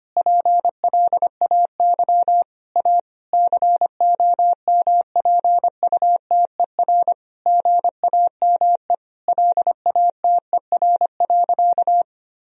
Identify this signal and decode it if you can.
Morse code, heard as a keyed tone, about 25 wpm, 700 Hz. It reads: PLAY A COMPUTER GAME LATER.
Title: Sentences from Top 1000 Words - Review Part 3 - 25wpm